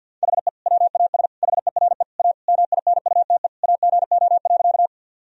Morse code, heard as a keyed tone, about 50 wpm, 700 Hz. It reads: HE WAS HERE A MINUTE AGO.
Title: Sentences from Top 1000 Words - Review Part 3 - 50wpm